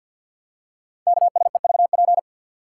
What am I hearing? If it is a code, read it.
Morse code, heard as a keyed tone, about 51 wpm, 700 Hz. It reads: KSEVP
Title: Code Groups - 5 Letter:Numbers - 50wpm